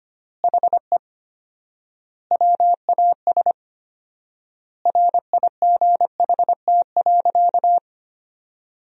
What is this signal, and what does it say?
Morse code, heard as a keyed tone, about 25 wpm, 700 Hz. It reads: HE WAS RIGHT.
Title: 4x Word-Spacing - Sentences from Top 500 Words - 25wpm